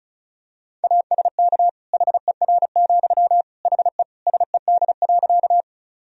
Morse code, heard as a keyed tone, about 34 wpm, 700 Hz. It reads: ASK HER, HE SED.
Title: Sentences from Top 200 Words and Abbreviations 35wpm